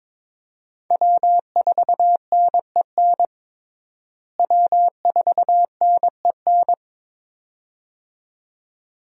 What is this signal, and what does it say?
Morse code, heard as a keyed tone, about 22 wpm, 700 Hz. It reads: W4NEN W4NEN
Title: USA Call Signs - 5 Characters - Repeated 2x - 22wpm